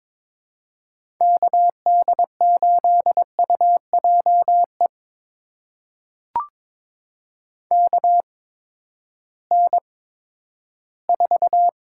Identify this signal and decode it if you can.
Morse code, heard as a keyed tone, about 21 wpm, spaced as for 12 wpm, 700 Hz. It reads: KD8UJE KN4
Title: USA Call Signs - 6 Characters - 8x Character Spacing - 22wpm